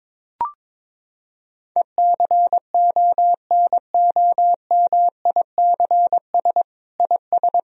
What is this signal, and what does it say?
Morse code, heard as a keyed tone, about 22 wpm, 700 Hz. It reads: ECONOMICS IS
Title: Sentences from Top 1000 Words - Review Part 3 - 22wpm